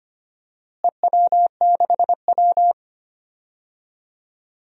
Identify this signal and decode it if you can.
Morse code, heard as a keyed tone, about 25 wpm, 700 Hz. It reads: EW6W
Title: DXCC-100 Call Signs - 4 Characters - Repeated 2x - 25wpm